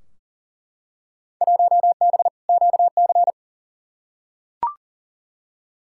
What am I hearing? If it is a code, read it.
Morse code, heard as a keyed tone, about 40 wpm, 700 Hz. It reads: 1B QC
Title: Field Day Exchanges Encore - 40wpm